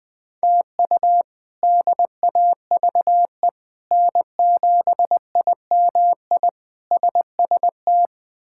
Morse code, heard as a keyed tone, about 20 wpm, 700 Hz. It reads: TU DAVE N7IMI SST
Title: K1USN SST QSO Practice - 20wpm